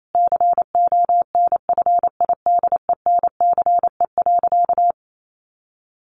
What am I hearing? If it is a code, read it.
Morse code, heard as a keyed tone, about 28 wpm, 700 Hz. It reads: CONFIDENCE.
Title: Sentences from Top 1500 Words 28wpm